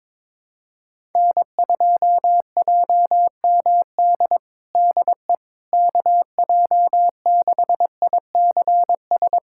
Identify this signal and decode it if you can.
Morse code, heard as a keyed tone, about 22 wpm, 700 Hz. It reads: N2JMD DE KJ6ICS
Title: USA Callsign DE Callsign 22wpm